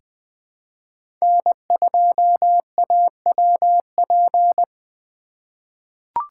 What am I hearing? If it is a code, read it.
Morse code, heard as a keyed tone, about 20 wpm, 700 Hz. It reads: N2AWP
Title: USA Call Signs - 5 Characters - 2x Character Spacing - 20wpm